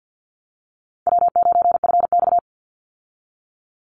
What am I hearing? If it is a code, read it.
Morse code, heard as a keyed tone, about 50 wpm, 700 Hz. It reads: W9FX
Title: Easy Call Signs - 50wpm